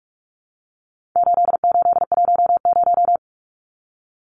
Morse code, heard as a keyed tone, about 44 wpm, 700 Hz. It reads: 8810
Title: Numbers - 4-Digit 45wpm